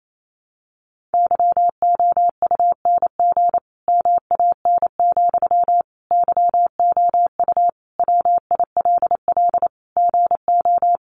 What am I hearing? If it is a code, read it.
Morse code, heard as a keyed tone, about 28 wpm, 700 Hz. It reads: YOUNG MAN, YOU WILL GO
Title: Sentences from Top 400 Words 28wpm